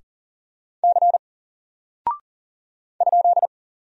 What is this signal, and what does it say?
Morse code, heard as a keyed tone, about 40 wpm, 700 Hz. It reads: C ?
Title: Single Character (Letter:Number:Punctuation:Prosign) 40wpm